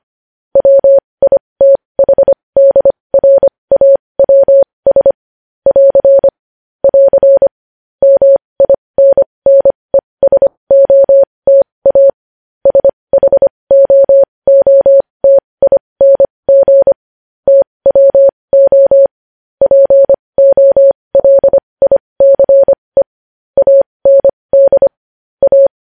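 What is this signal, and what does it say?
Morse code, heard as a keyed tone, about 25 wpm, spaced as for 20 wpm, 550 Hz. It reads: WITHDRAWS <AR> <AR> MINNESOTA SHOOTING TWO POLICE AND A
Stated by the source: Morse code news headlines sent at 20 wpm.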